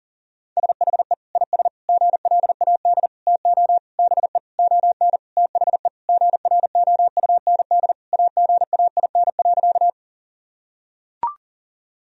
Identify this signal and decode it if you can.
Morse code, heard as a keyed tone, about 40 wpm, 700 Hz. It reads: SHE IS GLAD TO BE ON THE GROUND AGAIN.
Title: Sentences from Top 1000 Words - Review Part 2 - 40wpm